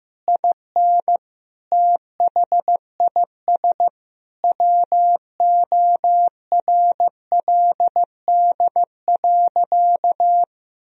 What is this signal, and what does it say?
Morse code, heard as a keyed tone, about 15 wpm, 700 Hz. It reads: IN THIS WORLD.
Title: Sentences from Top 1000 Words - Review Part 1 - 15wpm